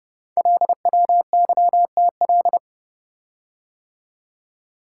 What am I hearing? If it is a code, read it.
Morse code, heard as a keyed tone, about 31 wpm, 690 Hz. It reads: LWYTL
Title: Code Groups - 5 Letters - 30wpm